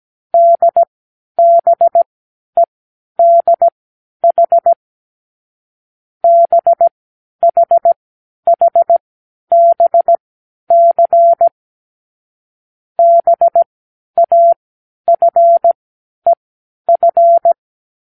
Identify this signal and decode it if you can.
Morse code, heard as a keyed tone, about 17 wpm, spaced as for 10 wpm, 680 Hz. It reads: DBEDH BHHBC BAFEF
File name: k7qo / Lesson12(A-H)